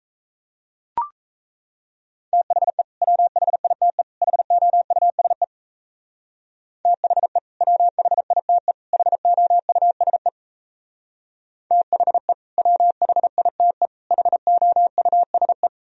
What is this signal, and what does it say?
Morse code, heard as a keyed tone, about 34 wpm, 700 Hz. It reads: THE WHITE HOUSE THE WHITE HOUSE THE WHITE HOUSE
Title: Speed-Racing - Sets of 3 Words 28wpm